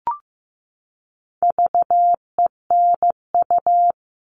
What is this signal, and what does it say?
Morse code, heard as a keyed tone, about 15 wpm, 700 Hz. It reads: VENU
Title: All Cut Numbers - 4-Digit 15wpm